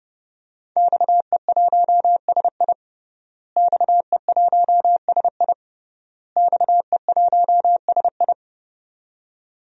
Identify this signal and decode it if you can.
Morse code, heard as a keyed tone, about 30 wpm, 700 Hz. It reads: XE1SI XE1SI XE1SI
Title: DXCC-100 Call Signs - 5 Characters - Repeated 3x - 30wpm